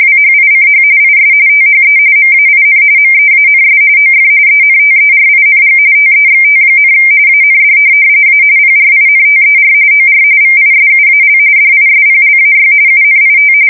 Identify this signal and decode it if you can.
RTTY, about 45.45 baud. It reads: WHICH IS THE MAIN MEANS OF ENTERING TEXT, AND A PRINTER OR VISUAL DISPLAY UNIT
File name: fsk170 / teletype-RTTY